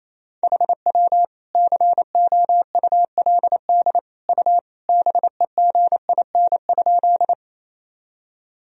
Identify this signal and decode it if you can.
Morse code, heard as a keyed tone, about 28 wpm, 700 Hz. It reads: HW COULD U BEGIN?
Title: Sentences from Top 300 Words and Abbreviations 28wpm